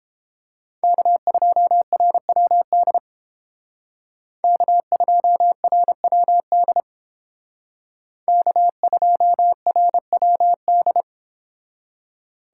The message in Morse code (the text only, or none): K2RWD K2RWD K2RWD